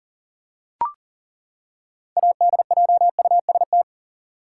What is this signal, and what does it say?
Morse code, heard as a keyed tone, about 40 wpm, 700 Hz. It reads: ADJUST
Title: Six Letter Words - 40wpm